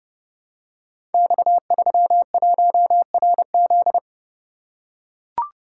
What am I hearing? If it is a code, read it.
Morse code, heard as a keyed tone, about 30 wpm, 700 Hz. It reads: X31RZ
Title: Code Groups - 5 Letter:Numbers - 30wpm